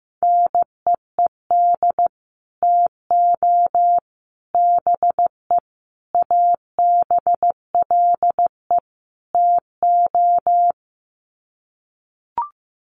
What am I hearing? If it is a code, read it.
Morse code, heard as a keyed tone, about 15 wpm, 700 Hz. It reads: NEED TO BE ABLE TO
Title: Sets of 5 Words Encore 15wpm